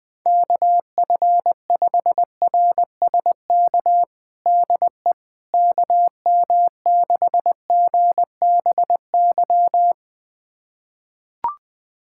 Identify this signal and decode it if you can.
Morse code, heard as a keyed tone, about 20 wpm, 700 Hz. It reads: KF5RSK DE KM6GBY
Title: USA Callsign DE Callsign - 6 Characters - 20wpm